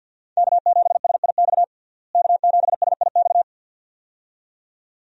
Morse code, heard as a keyed tone, about 52 wpm, 700 Hz. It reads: K7SIX K7SIX
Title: USA Call Signs - 5 Characters - Repeated 2x - 50wpm